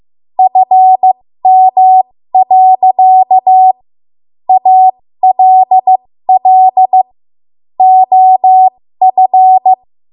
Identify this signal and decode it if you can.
Morse code, heard as a keyed tone, about 15 wpm, spaced as for 13 wpm, 750 Hz. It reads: FM. ALL OF
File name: Morse / 13WPM